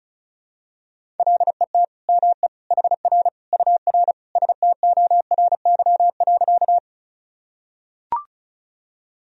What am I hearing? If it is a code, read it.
Morse code, heard as a keyed tone, about 36 wpm, 700 Hz. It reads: LET ME HR UR STORY.
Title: Sentences from Top 300 Words and Abbreviations 35wpm